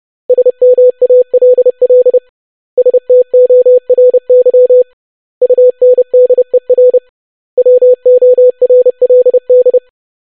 Morse code, heard as a keyed tone, about 30 wpm, 500 Hz. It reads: SMALL STORY UNDER WORLD
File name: basic500-25wpm